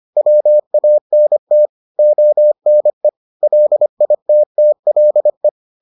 Morse code, heard as a keyed tone, about 25 wpm, 600 Hz. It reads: WANT ONE LITTLE